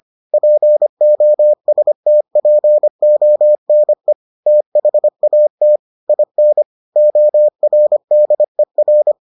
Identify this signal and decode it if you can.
Morse code, heard as a keyed tone, about 25 wpm, 600 Hz. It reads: POSTPONE THAT IN ORDER